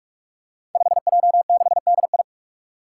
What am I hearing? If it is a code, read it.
Morse code, heard as a keyed tone, about 46 wpm, 700 Hz. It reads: 5J6DI
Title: Five Challenging Letter-Number Combinations 45wpm